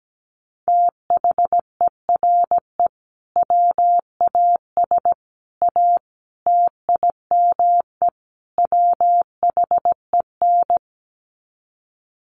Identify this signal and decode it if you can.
Morse code, heard as a keyed tone, about 17 wpm, 700 Hz. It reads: THERE WAS A TIME WHEN
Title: Sets of 5 Words 17wpm